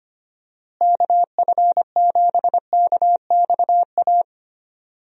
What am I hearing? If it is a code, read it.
Morse code, heard as a keyed tone, about 25 wpm, 700 Hz. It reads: KF7KXA
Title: USA Call Signs - 6 Characters - Repeated 3x - 25wpm